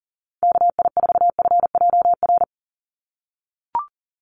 Morse code, heard as a keyed tone, about 40 wpm, 690 Hz. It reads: KI4FJR
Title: USA Call Signs 40wpm